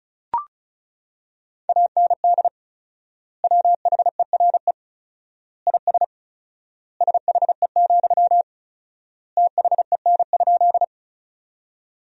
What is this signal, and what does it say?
Morse code, heard as a keyed tone, about 36 wpm, 700 Hz. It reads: AND WHERE IS SHE, THEN?
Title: 4x Word-Spacing - Sentences from Top 200 Words - 35wpm